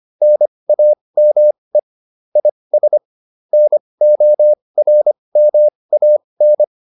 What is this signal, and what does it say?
Morse code, heard as a keyed tone, about 25 wpm, spaced as for 20 wpm, 600 Hz. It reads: NAME IS NORMAN